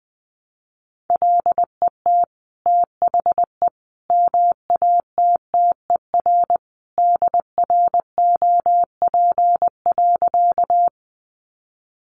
Morse code, heard as a keyed tone, about 20 wpm, 700 Hz. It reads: LET THE MATTER DROP.
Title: Sentences from Top 1400 Words 20wpm